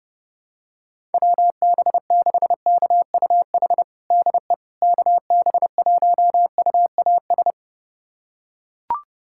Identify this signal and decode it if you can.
Morse code, heard as a keyed tone, about 30 wpm, 700 Hz. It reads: WB6KUH DE KB1UAS
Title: USA Callsign DE Callsign - 6 Characters - 30wpm